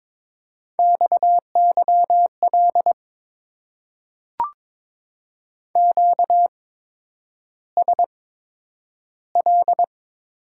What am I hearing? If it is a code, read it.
Morse code, heard as a keyed tone, about 22 wpm, spaced as for 9 wpm, 700 Hz. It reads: XYL QSL
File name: Top 90 QSO Elements - 8x Character Spacing - 22wpm